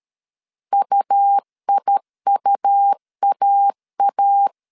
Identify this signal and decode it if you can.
Morse code, heard as a keyed tone, about 13 wpm, 800 Hz. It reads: UIUAA
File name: CW / DET / 1 3 1500C S B 12wpm 2828